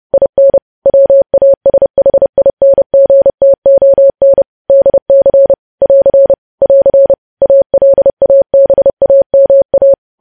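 Morse code, heard as a keyed tone, about 30 wpm, 550 Hz. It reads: IN WASHINGTON DC <AR> <AR> ALABAMA
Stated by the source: News headlines sent in Morse at 30 words per minute.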